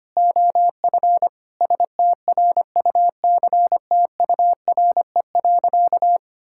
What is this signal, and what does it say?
Morse code, heard as a keyed tone, about 25 wpm, 700 Hz. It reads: OF STRUCTURE.
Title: Sentences from Top 1000 Words - Review Part 6 - 25wpm